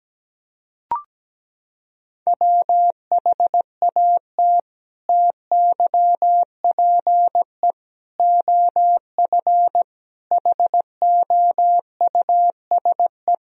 Morse code, heard as a keyed tone, about 17 wpm, 700 Hz. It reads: WHAT TYPE OF HOUSE
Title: Sentences from Top 1000 Words - Review Part 3 - 17wpm